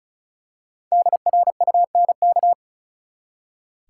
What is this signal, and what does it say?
Morse code, heard as a keyed tone, about 34 wpm, 700 Hz. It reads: DRUNK
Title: Five Letter Words - 35wpm